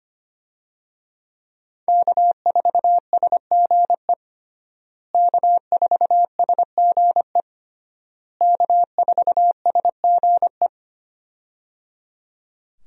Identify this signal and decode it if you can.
Morse code, heard as a keyed tone, about 25 wpm, 700 Hz. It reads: K4SGE K4SGE K4SGE
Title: USA Call Signs - 5 Characters - Repeated 3x - 25wpm